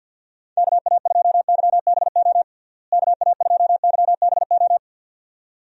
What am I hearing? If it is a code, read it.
Morse code, heard as a keyed tone, about 51 wpm, 700 Hz. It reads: KA2YBO KA2YBO
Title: USA Call Signs - 6 Characters - Repeated 2x - 50wpm